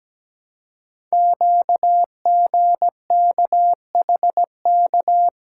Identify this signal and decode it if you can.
Morse code, heard as a keyed tone, about 17 wpm, 700 Hz. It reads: QGKHK
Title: Code Groups - 5 Letters - 17wpm